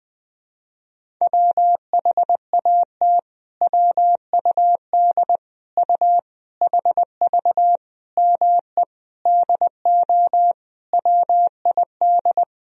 Morse code, heard as a keyed tone, about 20 wpm, 700 Hz. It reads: WHAT WUD U HV ME DO WID
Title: Sentences from Top 300 Words and Abbreviations 20wpm